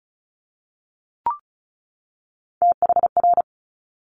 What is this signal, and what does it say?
Morse code, heard as a keyed tone, about 36 wpm, 700 Hz. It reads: THR
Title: Multi-letter Phonemes 35wpm